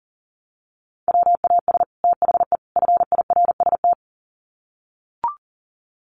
Morse code, heard as a keyed tone, about 40 wpm, 700 Hz. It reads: WAS THE FIRST
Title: Sets of 3 Words 40wpm